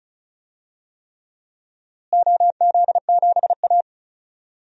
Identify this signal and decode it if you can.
Morse code, heard as a keyed tone, about 36 wpm, 700 Hz. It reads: OZ7A
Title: DXCC-100 Call Signs - 4 Characters - Repeated 3x - 35wpm